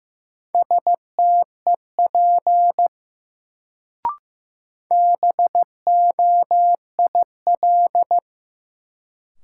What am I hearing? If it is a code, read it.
Morse code, heard as a keyed tone, about 15 wpm, 700 Hz. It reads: STEP BOIL
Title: Sets of 5 Words Encore - New Word Drill - 15wpm